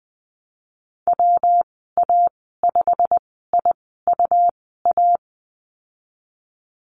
Morse code, heard as a keyed tone, about 20 wpm, spaced as for 15 wpm, 700 Hz. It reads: WA5IUA
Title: 147 Lesson 24g - Letter y - Call Signs 20wpm